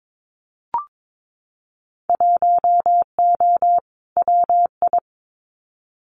Morse code, heard as a keyed tone, about 22 wpm, 700 Hz. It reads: 1O WI
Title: Winter Field Day Exchanges 22wpm